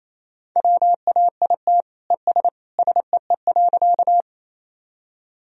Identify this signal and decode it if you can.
Morse code, heard as a keyed tone, about 28 wpm, 700 Hz. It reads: WAIT ES SEE.
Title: Sentences from Top 500 Words and Abbreviations 28wpm